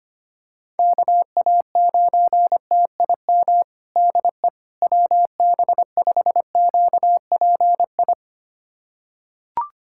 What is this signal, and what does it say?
Morse code, heard as a keyed tone, about 25 wpm, 700 Hz. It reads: KA9TIM DE WB5QPI
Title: USA Callsign DE Callsign - 6 Characters - 25wpm